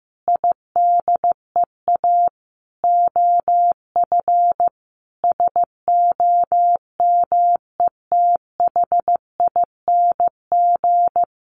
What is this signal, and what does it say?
Morse code, heard as a keyed tone, about 15 wpm, 700 Hz. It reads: IDEA OF SOMETHING
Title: Sentences from Top 1500 Words 15wpm